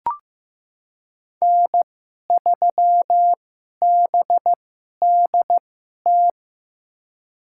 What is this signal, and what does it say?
Morse code, heard as a keyed tone, about 15 wpm, spaced as for 11 wpm, 700 Hz. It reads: N3BDT
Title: USA Call Signs - 5 Characters - 2x Character Spacing - 15wpm